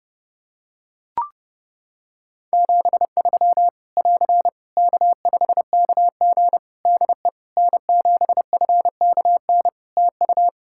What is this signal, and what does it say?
Morse code, heard as a keyed tone, about 30 wpm, 700 Hz. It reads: 73 <AR> K5KG DE N7FKN TU
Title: K1USN SST QSO Practice 30wpm